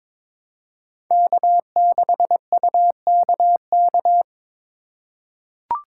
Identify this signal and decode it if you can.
Morse code, heard as a keyed tone, about 22 wpm, 700 Hz. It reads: K6UKK
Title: USA Call Signs - 5 Characters - 2x Character Spacing - 22wpm